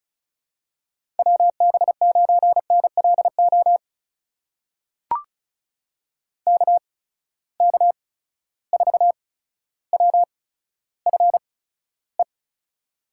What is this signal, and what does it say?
Morse code, heard as a keyed tone, about 36 wpm, spaced as for 15 wpm, 700 Hz. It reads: WB9NLO KK4WFE
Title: USA Call Signs - 6 Characters - 8x Character Spacing - 35wpm